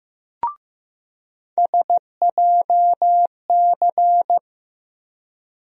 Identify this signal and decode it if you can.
Morse code, heard as a keyed tone, about 15 wpm, 700 Hz. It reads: SJC
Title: USA Call Sign Postfixes - 3 Letters 15wpm